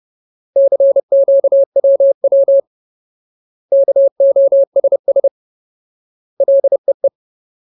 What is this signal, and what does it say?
Morse code, heard as a keyed tone, about 30 wpm, 550 Hz. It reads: CQWW KOSS LEE
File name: Ultimate Word List / 30WPM Ultimate